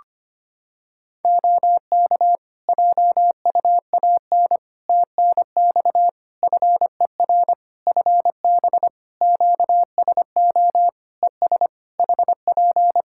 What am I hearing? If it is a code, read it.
Morse code, heard as a keyed tone, about 25 wpm, 700 Hz. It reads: OK JUAN TNX FER FB QSO ES HP
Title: K1USN SST QSO Practice 25wpm